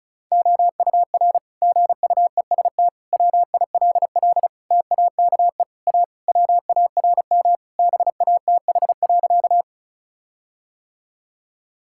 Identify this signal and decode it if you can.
Morse code, heard as a keyed tone, about 34 wpm, 700 Hz. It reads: OUR GUEST WILL TAKE A WARM BATH.
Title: Sentences from Top 1600 Words 35wpm